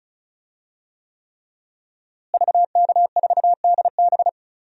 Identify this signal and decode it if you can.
Morse code, heard as a keyed tone, about 36 wpm, 700 Hz. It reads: VK4DB
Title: DXCC-100 Call Signs - 5 Characters - Repeated 3x - 35wpm